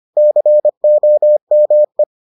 Morse code, heard as a keyed tone, about 25 wpm, 600 Hz. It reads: COME